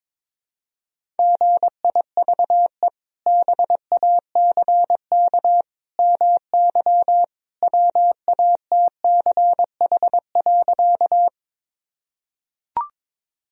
Morse code, heard as a keyed tone, about 22 wpm, 700 Hz. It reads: GIVE BACK MY WATCH.
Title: Sentences from Top 1000 Words - Review Part 1 - 22wpm